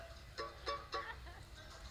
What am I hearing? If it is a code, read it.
There is no code.